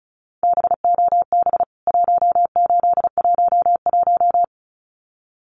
Morse code, heard as a keyed tone, about 34 wpm, 700 Hz. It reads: BOB 1811
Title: CWT Exchange 35wpm